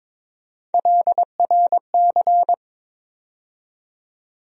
Morse code, heard as a keyed tone, about 22 wpm, 700 Hz. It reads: LRC
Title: USA Call Sign Postfixes - 3 Letters 22wpm